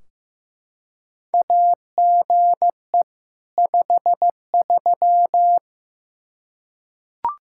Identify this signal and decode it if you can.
Morse code, heard as a keyed tone, about 15 wpm, 700 Hz. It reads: AGE 53
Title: K1USN SST QSO Practice 15wpm